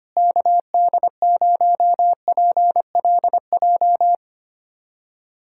Morse code, heard as a keyed tone, about 25 wpm, 700 Hz. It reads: KD0PLJ
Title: USA Callsign DE Callsign - 6 Characters - 25wpm